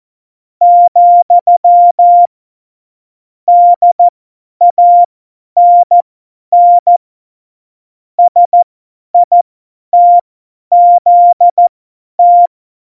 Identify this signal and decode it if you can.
Morse code, heard as a keyed tone, about 14 wpm, spaced as for 10 wpm, 700 Hz. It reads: , DANN SITZT
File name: Morsekurs / Witze / Witz 25. Dezember, 50 F70